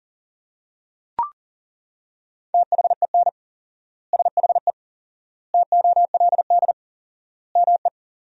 Morse code, heard as a keyed tone, about 40 wpm, 700 Hz. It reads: THEN SHE TOLD ME
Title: 4x Word-Spacing - Sentences from Top 400 Words - 40wpm